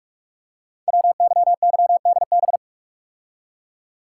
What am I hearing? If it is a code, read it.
Morse code, heard as a keyed tone, about 44 wpm, 700 Hz. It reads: WYYDB
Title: Code Groups - 5 Letters - 45wpm